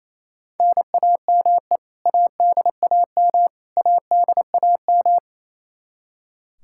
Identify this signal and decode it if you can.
Morse code, heard as a keyed tone, about 28 wpm, 700 Hz. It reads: NAME ADAM ADAM
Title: K1USN SST QSO Practice 28wpm